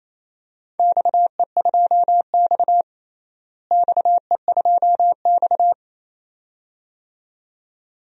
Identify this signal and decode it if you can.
Morse code, heard as a keyed tone, about 28 wpm, 700 Hz. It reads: XE2X XE2X
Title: DXCC-100 Call Signs - 4 Characters - Repeated 2x - 28wpm